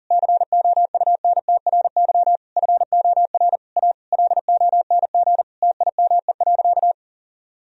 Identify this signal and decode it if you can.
Morse code, heard as a keyed tone, about 40 wpm, 700 Hz. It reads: COUNTRY FOR A LONG TIME.
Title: Sentences from Top 1000 Words - Review Part 1 - 40wpm